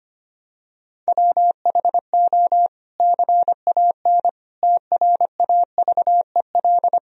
Morse code, heard as a keyed tone, about 25 wpm, 700 Hz. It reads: WHO CAN TRAVEL
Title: Sentences from Top 1000 Words - Review Part 3 - 25wpm